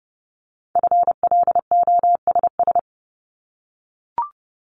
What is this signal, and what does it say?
Morse code, heard as a keyed tone, about 30 wpm, 690 Hz. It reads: FLOSS
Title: Multi-letter Phonemes 30wpm